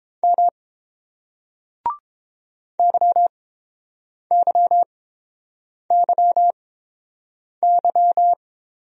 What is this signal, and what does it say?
Morse code, heard as a keyed tone, about 29 wpm, spaced as for 7 wpm, 700 Hz. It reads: M YYYY